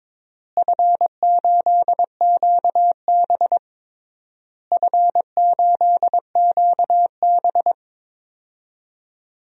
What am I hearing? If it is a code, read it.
Morse code, heard as a keyed tone, about 22 wpm, 700 Hz. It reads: F8QB F8QB
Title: DXCC-100 Call Signs - 4 Characters - Repeated 3x - 22wpm